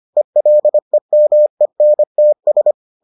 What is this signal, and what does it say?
Morse code, heard as a keyed tone, about 25 wpm, 600 Hz. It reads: ELEMENTS